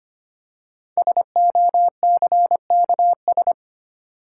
Morse code, heard as a keyed tone, about 25 wpm, 700 Hz. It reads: SOCKS
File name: practice / 25wpm / Easy Words with Hints - 25wpm